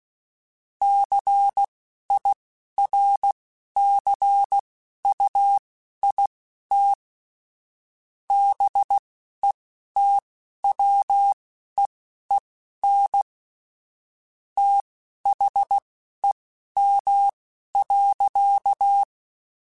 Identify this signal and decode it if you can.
Morse code, heard as a keyed tone, about 16 wpm, spaced as for 11 wpm, 800 Hz. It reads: CIRCUIT BETWEEN THEM.
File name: Technical Subjects / Technical Subjects-mp3s / Sw Pwr Sup @10